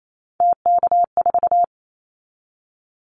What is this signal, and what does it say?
Morse code, heard as a keyed tone, about 28 wpm, 700 Hz. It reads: TK4
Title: Callsign Prefixes for Top 100 DXCC - 28wpm